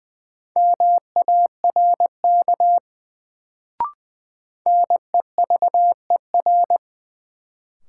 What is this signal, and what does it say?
Morse code, heard as a keyed tone, about 20 wpm, 700 Hz. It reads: MARK NEVER
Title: Sentences from Top 300 Words - New Word Drill - 20wpm